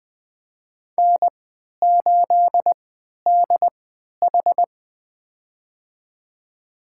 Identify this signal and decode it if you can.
Morse code, heard as a keyed tone, about 20 wpm, spaced as for 11 wpm, 700 Hz. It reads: N8DH
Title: USA Call Signs - 3x Character Spacing - 20wpm